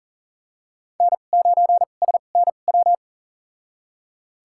Morse code, heard as a keyed tone, about 39 wpm, spaced as for 28 wpm, 700 Hz. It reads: N9SNW
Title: USA Call Signs - 5 Characters - 2x Character Spacing - 40wpm